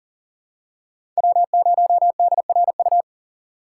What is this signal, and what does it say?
Morse code, heard as a keyed tone, about 40 wpm, 700 Hz. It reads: W0DRU
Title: USA Call Signs - 6x Character Spacing - 40wpm